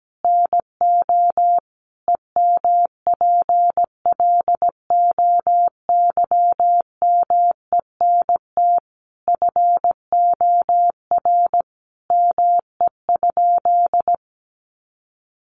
Morse code, heard as a keyed tone, about 17 wpm, 700 Hz. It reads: NO EMPLOYMENT FOR ME?
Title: Sentences from Top 1500 Words 17wpm